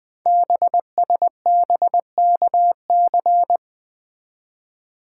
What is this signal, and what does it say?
Morse code, heard as a keyed tone, about 20 wpm, 700 Hz. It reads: BSBKC
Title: Code Groups - 5 Letters - 20wpm